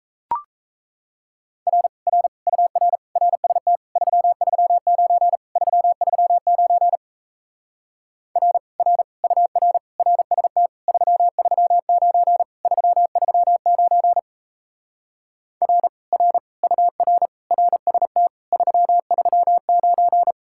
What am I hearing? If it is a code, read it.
Morse code, heard as a keyed tone, about 37 wpm, 700 Hz. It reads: R R UR RST 339 339 R R UR RST 339 339 R R UR RST 339